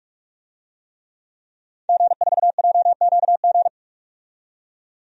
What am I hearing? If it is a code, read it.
Morse code, heard as a keyed tone, about 43 wpm, 700 Hz. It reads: G4JQG
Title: DXCC-100 Call Signs - 5 Characters - 45wpm